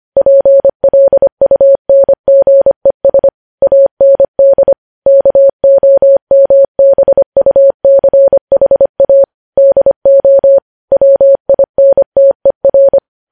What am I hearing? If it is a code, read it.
Morse code, heard as a keyed tone, about 25 wpm, 550 Hz. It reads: PLUNGES AND KOMBUCHA DO WINTER
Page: News Headlines in Morse Code at 25 WPM